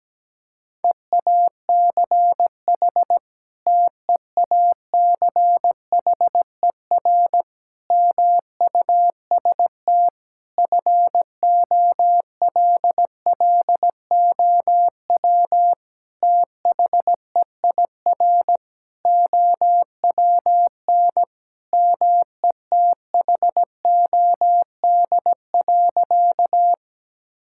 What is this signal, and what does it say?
Morse code, heard as a keyed tone, about 17 wpm, 700 Hz. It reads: EACH TEACHER MUST FOLLOW THEIR OWN METHOD.
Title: Sentences from Top 1000 Words - Review Part 6 - 17wpm